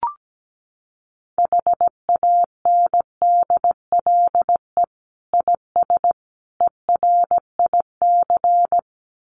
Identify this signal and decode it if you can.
Morse code, heard as a keyed tone, about 17 wpm, 700 Hz. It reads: HANDLE IS ERIC
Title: Common QSO Phrases 17wpm